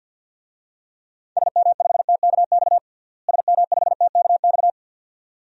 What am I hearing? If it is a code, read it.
Morse code, heard as a keyed tone, about 47 wpm, 700 Hz. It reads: SM5TKX SM5TKX
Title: DXCC-100 Call Signs - 6 Characters - Repeated 2x - 50wpm